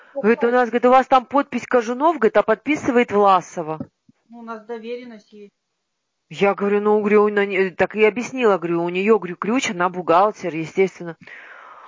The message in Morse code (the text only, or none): none